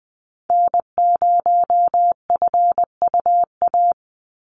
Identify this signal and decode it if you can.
Morse code, heard as a keyed tone, about 20 wpm, 700 Hz. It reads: N0FUA
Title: USA Call Signs 20wpm